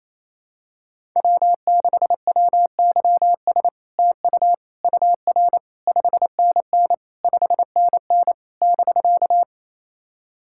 Morse code, heard as a keyed tone, about 28 wpm, 700 Hz. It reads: W6WYS TU UR 5NN 5NN <BK>
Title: SOTA QSO Practice 28wpm